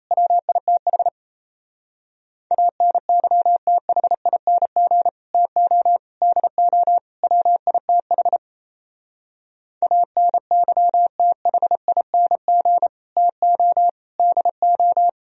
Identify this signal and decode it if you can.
Morse code, heard as a keyed tone, about 38 wpm, 700 Hz. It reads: WITH ANYTHING TO DO WITH ANYTHING TO DO